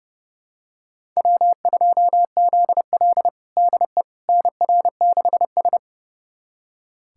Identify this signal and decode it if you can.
Morse code, heard as a keyed tone, about 32 wpm, 700 Hz. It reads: W2ZL DE NR6S
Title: USA Callsign DE Callsign - 4 Characters - 30wpm